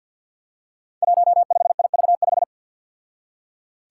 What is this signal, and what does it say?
Morse code, heard as a keyed tone, about 53 wpm, 700 Hz. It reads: 15IV5
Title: Five Challenging Letter-Number Combinations 50wpm